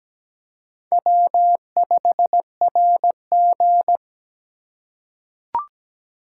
Morse code, heard as a keyed tone, about 17 wpm, 700 Hz. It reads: W5RG
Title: USA Call Signs - 4 Characters - 2x Character Spacing - 17wpm